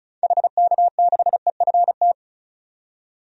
Morse code, heard as a keyed tone, about 36 wpm, 700 Hz. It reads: HK6EFT
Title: DXCC-100 Call Signs - 6 Characters - 35wpm